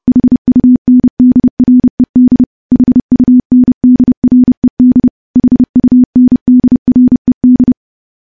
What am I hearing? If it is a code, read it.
Morse code, heard as a keyed tone, about 30 wpm, 250 Hz. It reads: HUNDRED HUNDRED HUNDRED
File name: morse / morse-training-30WPM